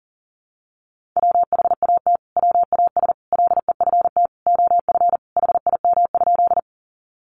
Morse code, heard as a keyed tone, about 40 wpm, 700 Hz. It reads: WHAT WAS LEFT OF HIM?
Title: Sentences from Top 300 Words 40wpm